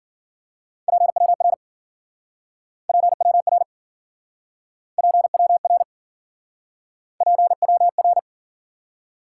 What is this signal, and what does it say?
Morse code, heard as a keyed tone, about 52 wpm, 700 Hz. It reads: PWR PWR PWR PWR